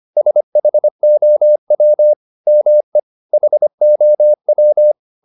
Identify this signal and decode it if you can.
Morse code, heard as a keyed tone, about 25 wpm, 600 Hz. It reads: SHOW ME HOW
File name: the old man and the sea part 1 25wpm